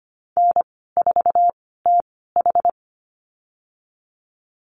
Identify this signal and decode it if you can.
Morse code, heard as a keyed tone, about 25 wpm, spaced as for 16 wpm, 700 Hz. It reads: N4TH
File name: Lesson 15g - Number 5 - Call Signs 25wpm